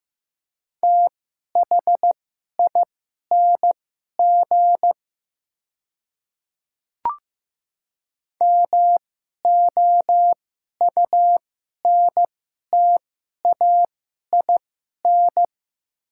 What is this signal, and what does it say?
Morse code, heard as a keyed tone, about 15 wpm, spaced as for 11 wpm, 700 Hz. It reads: THING MOUNTAIN
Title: 2x Character Spacing - Top 300 Words - 15wpm